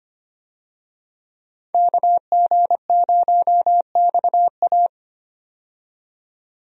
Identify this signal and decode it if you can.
Morse code, heard as a keyed tone, about 25 wpm, 700 Hz. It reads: KG0XA
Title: USA Call Signs - 5 Characters - 25wpm